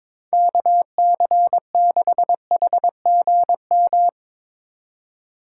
Morse code, heard as a keyed tone, about 24 wpm, 700 Hz. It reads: KC6HGM
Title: Speed Racing - USA 2x3 Call Signs 22wpm